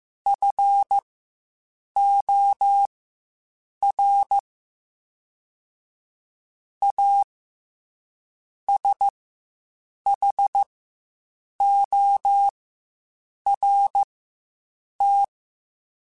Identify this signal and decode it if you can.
Morse code, heard as a keyed tone, about 15 wpm, spaced as for 7 wpm, 800 Hz. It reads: FOR A SHORT
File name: Quotes Spaced / Quotes Spaced-mp3s / Temper sp @10